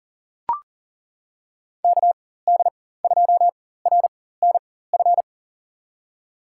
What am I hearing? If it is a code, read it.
Morse code, heard as a keyed tone, about 40 wpm, spaced as for 19 wpm, 700 Hz. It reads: KD2RNF
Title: USA Call Signs - 6 Characters - 4x Character Spacing - 40wpm